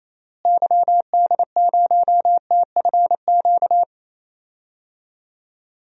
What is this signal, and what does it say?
Morse code, heard as a keyed tone, about 28 wpm, 700 Hz. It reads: YD0TFQ
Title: DXCC-100 Call Signs - 6 Characters - Repeated 3x - 28wpm